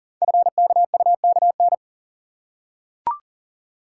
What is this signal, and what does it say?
Morse code, heard as a keyed tone, about 39 wpm, 690 Hz. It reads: FKUKN
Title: Code Groups - 5 Letters - 40wpm